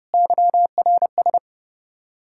Morse code, heard as a keyed tone, about 38 wpm, 700 Hz. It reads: YRS